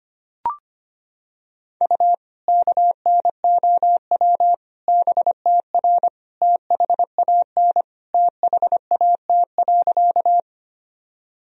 Morse code, heard as a keyed tone, about 25 wpm, 700 Hz. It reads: U KNOW BTR THAN THAT.
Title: Sentences from Top 400 Words and Abbreviations 25wpm